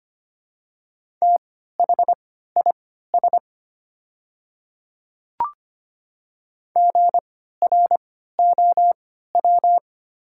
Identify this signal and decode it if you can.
Morse code, heard as a keyed tone, about 25 wpm, spaced as for 14 wpm, 700 Hz. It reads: THIS GROW
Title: 3x Character Spacing - Top 500 Words - 25wpm